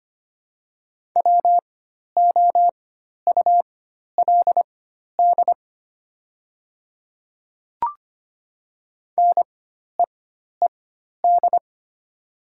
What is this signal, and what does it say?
Morse code, heard as a keyed tone, about 25 wpm, spaced as for 12 wpm, 690 Hz. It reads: WOULD NEED
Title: 4x Character Spacing - Top 200 Words - 25wpm